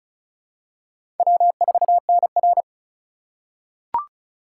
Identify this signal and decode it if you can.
Morse code, heard as a keyed tone, about 36 wpm, 700 Hz. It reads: W4NR
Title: USA Call Signs - 4 Characters - 2x Character Spacing - 35wpm